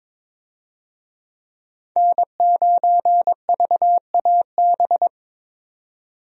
Morse code, heard as a keyed tone, about 22 wpm, 700 Hz. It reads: N9VAB
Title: USA Call Signs - 5 Characters - 22wpm